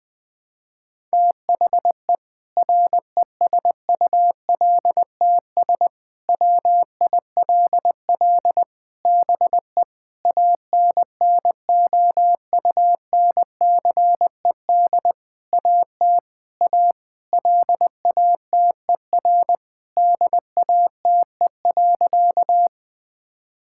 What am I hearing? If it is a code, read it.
Morse code, heard as a keyed tone, about 20 wpm, 700 Hz. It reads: THE RESULTS WILL BE ANNOUNCED AT A LATER DATE.
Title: Sentences with 7-Word Phrases - 20wpm